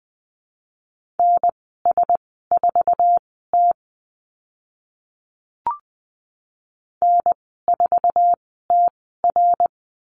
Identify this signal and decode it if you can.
Morse code, heard as a keyed tone, about 20 wpm, spaced as for 14 wpm, 700 Hz. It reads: NS4T N4TR
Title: 069 Lesson 12g - Letter d - Call Signs 20wpm